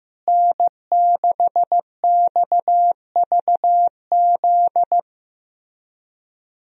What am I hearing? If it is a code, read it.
Morse code, heard as a keyed tone, about 15 wpm, 700 Hz. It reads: N6XVZ
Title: USA Call Signs - 5 Characters - Repeated 3x - 15wpm